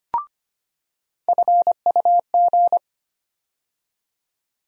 Morse code, heard as a keyed tone, about 25 wpm, 700 Hz. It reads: FUG
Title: USA Call Sign Postfixes - 3 Letters 25wpm